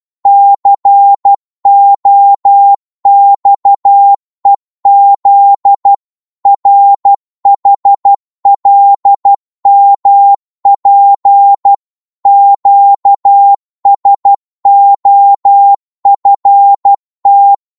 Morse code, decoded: COXEZ RHLMP QSOFT